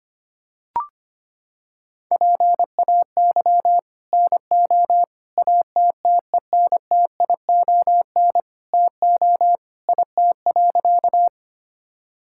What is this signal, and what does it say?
Morse code, heard as a keyed tone, about 25 wpm, 700 Hz. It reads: PAY NO ATTENTION TO IT.
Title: Sentences from Top 1000 Words - Review Part 6 - 25wpm